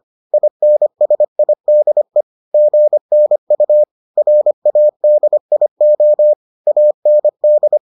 Morse code, heard as a keyed tone, about 25 wpm, 600 Hz. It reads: INSIDE GNU RADIO AND